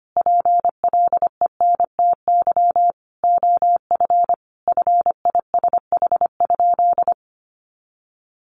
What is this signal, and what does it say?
Morse code, heard as a keyed tone, about 25 wpm, 700 Hz. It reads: PLENTY OF FISH?
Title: Sentences from Top 1500 Words 25wpm